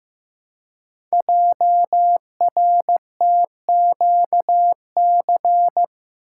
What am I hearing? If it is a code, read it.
Morse code, heard as a keyed tone, about 15 wpm, 700 Hz. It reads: JRTQC
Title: Code Groups - 5 Letters - 15wpm